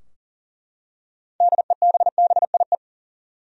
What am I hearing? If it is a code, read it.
Morse code, heard as a keyed tone, about 40 wpm, 700 Hz. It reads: DEBBIE
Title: USA Common Names - 40wpm